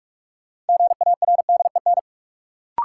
Morse code, heard as a keyed tone, about 44 wpm, 700 Hz. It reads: GARDEN
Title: Sentences from Top 700 Words - New Word Drill - 45wpm